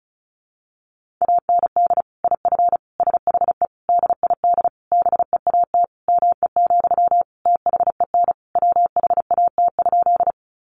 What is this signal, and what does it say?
Morse code, heard as a keyed tone, about 34 wpm, 700 Hz. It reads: AND IF SHE DID BEAT ME, THEN WHAT?
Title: Sentences from Top 1300 Words 35wpm